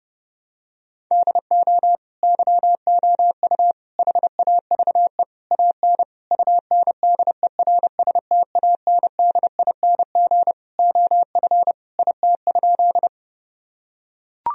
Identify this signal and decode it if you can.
Morse code, heard as a keyed tone, about 29 wpm, 700 Hz. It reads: DO YOU HAVE AN UNDERSTANDING OF IT?
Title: Sentences from Top 1000 Words - Review Part 3 - 30wpm